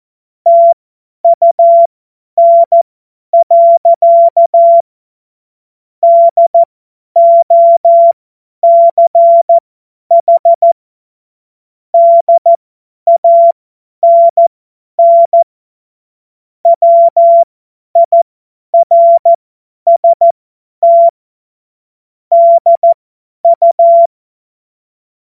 Morse code, heard as a keyed tone, about 14 wpm, spaced as for 10 wpm, 680 Hz. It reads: TUN. DOCH DANN WIRST DU
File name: Morsekurs / Witze / Witz 29. Juni 2021, 50 F70